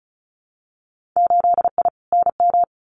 Morse code, heard as a keyed tone, about 36 wpm, 700 Hz. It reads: 8I NM
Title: Winter Field Day Exchanges Encore 35wpm